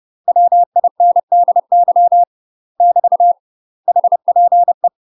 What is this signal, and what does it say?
Morse code, heard as a keyed tone, about 30 wpm, 700 Hz. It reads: WINDY <BT> HPE
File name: UK MASTER QSOs TRAINING / 30WPM / UK MASTER QSOs 0050